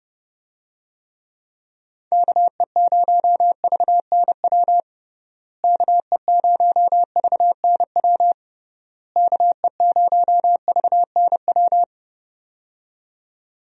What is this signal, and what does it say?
Morse code, heard as a keyed tone, about 30 wpm, 700 Hz. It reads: KE0VNW KE0VNW KE0VNW
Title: USA Call Signs - 6 Characters - Repeated 3x - 30wpm